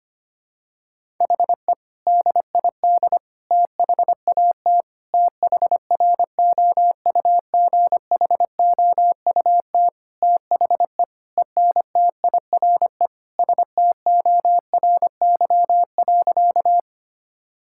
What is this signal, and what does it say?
Morse code, heard as a keyed tone, about 25 wpm, 700 Hz. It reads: HE DID THAT THROUGHOUT THE ENTIRE STORY.
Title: Sentences from Top 1000 Words - Review Part 5 - 25wpm